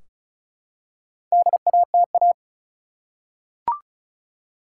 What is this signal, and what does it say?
Morse code, heard as a keyed tone, about 36 wpm, 700 Hz. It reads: DATA
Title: Four Letter Words - 35wpm